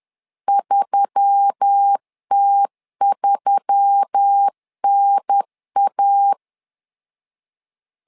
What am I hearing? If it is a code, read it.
Morse code, heard as a keyed tone, about 10 wpm, 800 Hz. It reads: 3T3NA
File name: LEZIONI / 4 Q1S3 10wpm 2427